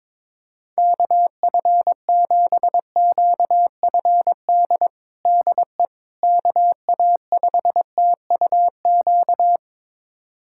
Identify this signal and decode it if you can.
Morse code, heard as a keyed tone, about 22 wpm, 700 Hz. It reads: KF7QFD DE KA5TUQ
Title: USA Callsign DE Callsign - 6 Characters - 22wpm